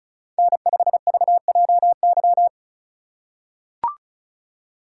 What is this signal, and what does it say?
Morse code, heard as a keyed tone, about 36 wpm, 700 Hz. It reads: N5VJY
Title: USA Call Signs - 5 Characters - 4x Character Spacing - 35wpm